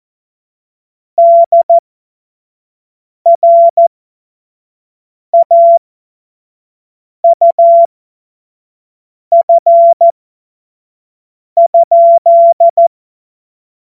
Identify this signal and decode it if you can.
Morse code, heard as a keyed tone, about 14 wpm, spaced as for 5 wpm, 680 Hz. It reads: DRAUF?
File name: Morsekurs / Witze / Witz 06. April 2021, 25 F70